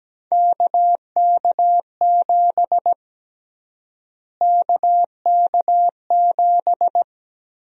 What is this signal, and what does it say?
Morse code, heard as a keyed tone, about 17 wpm, 700 Hz. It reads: KK7 KK7
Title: USA Call Sign Prefixes - Repeated 2x - 17wpm